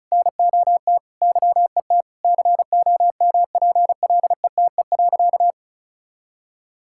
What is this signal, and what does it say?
Morse code, heard as a keyed tone, about 34 wpm, 700 Hz. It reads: NOT YET COMPLETE.
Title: Sentences from Top 1000 Words - Review Part 6 - 35wpm